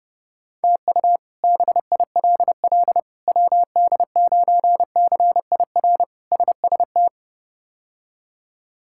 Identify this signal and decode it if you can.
Morse code, heard as a keyed tone, about 30 wpm, 700 Hz. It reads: TU BILL WD9CIR SST
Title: K1USN SST QSO Practice - 30wpm